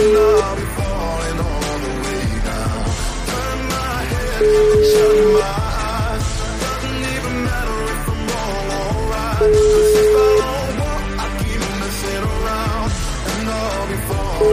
Ringback tone (Europe, 425 Hz), with noise.